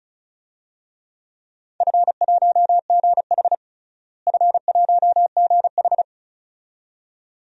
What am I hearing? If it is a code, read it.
Morse code, heard as a keyed tone, about 34 wpm, 700 Hz. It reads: F1GH F1GH
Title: DXCC-100 Call Signs - 4 Characters - Repeated 2x - 35wpm